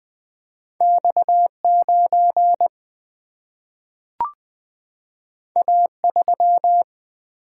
Morse code, heard as a keyed tone, about 20 wpm, 700 Hz. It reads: X 9 A 3
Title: Single-Letter Single-Number 20wpm